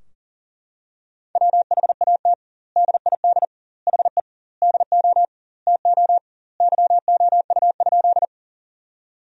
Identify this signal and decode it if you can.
Morse code, heard as a keyed tone, about 40 wpm, 700 Hz. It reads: WHAT DID HE DO TO YOU?
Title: 2x Word-Spacing - Sentences from Top 100 Words - 40wpm